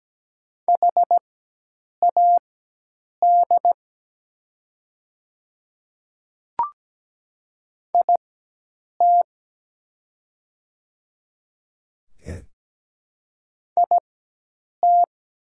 Morse code, heard as a keyed tone, about 17 wpm, spaced as for 8 wpm, 690 Hz. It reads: HAD IT IT
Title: 4x Character Spacing - Top 100 Words - 17wpm